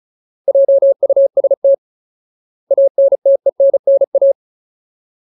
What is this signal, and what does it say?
Morse code, heard as a keyed tone, about 34 wpm, 550 Hz. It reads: JUST ANTENNA
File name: Ultimate Word List / 35WPM Ultimate